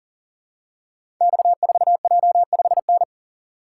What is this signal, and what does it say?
Morse code, heard as a keyed tone, about 42 wpm, 700 Hz. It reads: X4J5N
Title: Code Groups - 5 Letter:Numbers - 40wpm